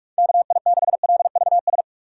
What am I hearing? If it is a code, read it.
Morse code, heard as a keyed tone, about 43 wpm, 700 Hz. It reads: KI6LVS
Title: USA Call Signs - 3x Character Spacing - 45wpm